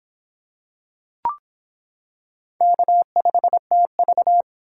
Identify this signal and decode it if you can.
Morse code, heard as a keyed tone, about 26 wpm, 700 Hz. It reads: K5TV